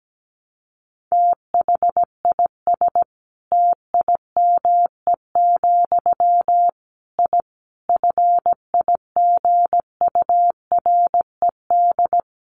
Morse code, heard as a keyed tone, about 17 wpm, 700 Hz. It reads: THIS TIME, I FIGURED
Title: A Sound Of Thunder 17wpm